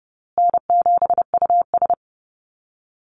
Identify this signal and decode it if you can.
Morse code, heard as a keyed tone, about 30 wpm, 700 Hz. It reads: N7US
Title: Call Signs Repeated 2x - 30wpm